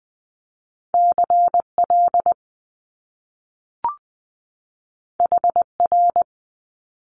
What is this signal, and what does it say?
Morse code, heard as a keyed tone, about 20 wpm, 700 Hz. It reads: C L H R
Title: Top Two Letter Combinations in Words 20wpm